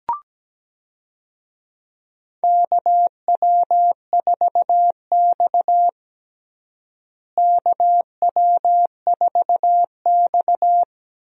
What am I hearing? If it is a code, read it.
Morse code, heard as a keyed tone, about 17 wpm, 700 Hz. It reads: KW4X KW4X
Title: USA Call Signs - 4 Characters - Repeated 2x - 17wpm